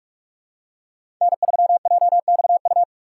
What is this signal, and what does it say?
Morse code, heard as a keyed tone, about 46 wpm, 700 Hz. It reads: N3JXU
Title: USA Call Signs - 5 Characters - 4x Character Spacing - 45wpm